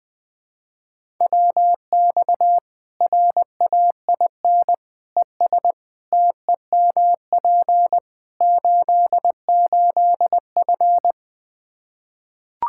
Morse code, heard as a keyed tone, about 20 wpm, 700 Hz. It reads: WX RAIN ES TEMP 88F
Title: Practice QSO - Exchange 2 - Info - 20wpm